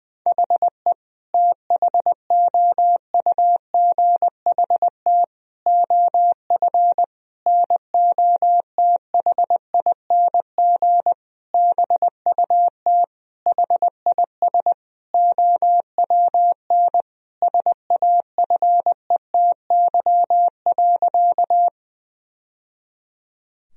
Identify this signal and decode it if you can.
Morse code, heard as a keyed tone, about 21 wpm, 700 Hz. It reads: HE THOUGHT OF NOTHING BUT HIS OWN SAFETY.
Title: Sentences from Top 1000 Words - Review Part 4 - 20wpm